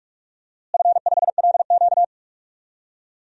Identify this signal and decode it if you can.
Morse code, heard as a keyed tone, about 45 wpm, 700 Hz. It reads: F5LQ
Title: DXCC-100 Call Signs - 4 Characters - Repeated 3x - 45wpm